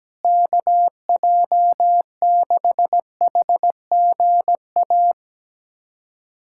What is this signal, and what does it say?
Morse code, heard as a keyed tone, about 17 wpm, 700 Hz. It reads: KJ6HGA
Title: USA Callsign DE Callsign - 6 Characters - 17wpm